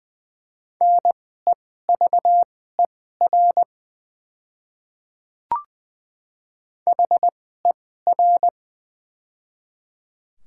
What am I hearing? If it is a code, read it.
Morse code, heard as a keyed tone, about 20 wpm, spaced as for 14 wpm, 700 Hz. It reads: NEVER HER
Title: 2x Character Spacing - Top 300 Words - 20wpm